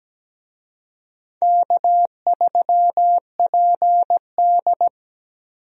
Morse code, heard as a keyed tone, about 17 wpm, 700 Hz. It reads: K3PD
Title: USA Call Signs - 4 Characters - Repeated 3x - 17wpm